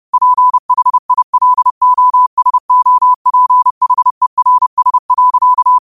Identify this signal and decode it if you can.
Morse code, heard as a keyed tone, about 30 wpm, 1000 Hz. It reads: PHILOSOPHERS.
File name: qotdfiles / 30wpm